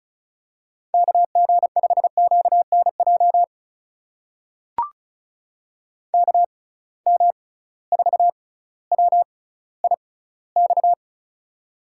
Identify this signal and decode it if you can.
Morse code, heard as a keyed tone, about 34 wpm, spaced as for 18 wpm, 700 Hz. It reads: KG5QNJ KM4WIX
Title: USA Call Signs - 6 Characters - 6x Character Spacing - 35wpm